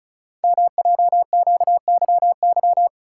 Morse code, heard as a keyed tone, about 36 wpm, 700 Hz. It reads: MJQYY
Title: Code Groups - 5 Letters - 35wpm